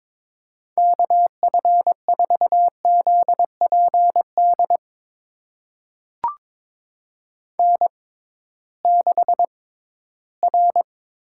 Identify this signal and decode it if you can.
Morse code, heard as a keyed tone, about 21 wpm, spaced as for 14 wpm, 700 Hz. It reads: KF4ZPD N6R
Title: USA Call Signs - 6x Character Spacing - 22wpm